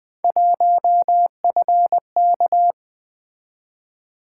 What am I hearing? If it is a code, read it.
Morse code, heard as a keyed tone, about 20 wpm, 700 Hz. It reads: 1FK
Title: Three Letter-Number Combinations - 20wpm